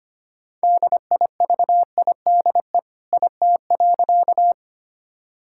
Morse code, heard as a keyed tone, about 25 wpm, 700 Hz. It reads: DIVIDE IT.
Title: Sentences from Top 1600 Words 25wpm